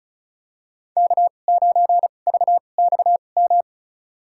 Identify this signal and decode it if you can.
Morse code, heard as a keyed tone, about 34 wpm, spaced as for 25 wpm, 700 Hz. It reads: K9VXM
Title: USA Call Signs - 5 Characters - 2x Character Spacing - 35wpm